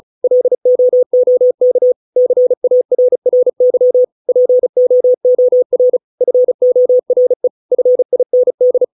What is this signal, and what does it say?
Morse code, heard as a keyed tone, about 35 wpm, 500 Hz. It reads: LOOK CARRY POOR FORE FIND